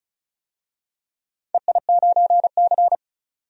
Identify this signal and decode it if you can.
Morse code, heard as a keyed tone, about 36 wpm, 700 Hz. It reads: EI9C
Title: DXCC-100 Call Signs - 4 Characters - 35wpm